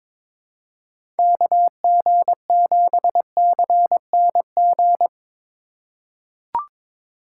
Morse code, heard as a keyed tone, about 23 wpm, 700 Hz. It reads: KG7CNG
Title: USA Call Signs - 6 Characters - 4x Character Spacing - 22wpm